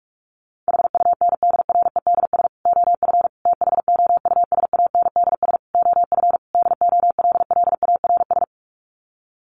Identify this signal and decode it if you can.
Morse code, heard as a keyed tone, about 44 wpm, 700 Hz. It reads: HUNDREDS OF THOUSANDS OF DOLLARS
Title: Sets of 5 Words Encore 45wpm